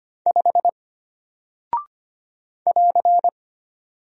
Morse code, heard as a keyed tone, about 25 wpm, 700 Hz. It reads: 5 <AR>
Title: Single Character (Letter:Number:Punctuation:Prosign) Alt 25wpm